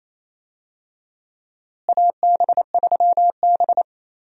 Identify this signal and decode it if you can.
Morse code, heard as a keyed tone, about 28 wpm, 700 Hz. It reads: AB3B
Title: USA Call Signs - 4 Characters - Repeated 3x - 28wpm